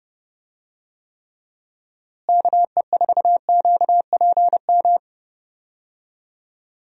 Morse code, heard as a keyed tone, about 30 wpm, 700 Hz. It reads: KE4QPM
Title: USA Call Signs - 6 Characters - Repeated 3x - 30wpm